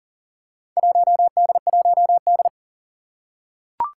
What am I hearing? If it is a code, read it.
Morse code, heard as a keyed tone, about 40 wpm, 700 Hz. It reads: 1D1D
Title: Four Challenging Letter-Number Combinations 40wpm